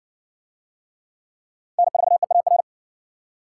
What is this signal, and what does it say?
Morse code, heard as a keyed tone, about 60 wpm, 700 Hz. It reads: N4EAR